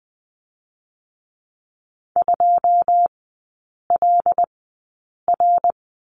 Morse code, heard as a keyed tone, about 20 wpm, spaced as for 8 wpm, 690 Hz. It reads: 2LR
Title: 087 Lesson 15e - Number 5 - Three Characters 20wpm